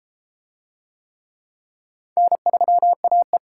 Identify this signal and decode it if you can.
Morse code, heard as a keyed tone, about 31 wpm, 700 Hz. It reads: N3AE